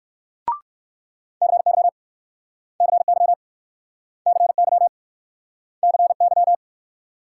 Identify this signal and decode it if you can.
Morse code, heard as a keyed tone, about 62 wpm, 700 Hz. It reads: CY CY CY CY